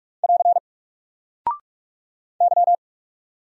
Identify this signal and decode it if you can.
Morse code, heard as a keyed tone, about 44 wpm, 700 Hz. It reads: <AR> Y
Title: Single Character (Letter:Number:Punctuation:Prosign) 45wpm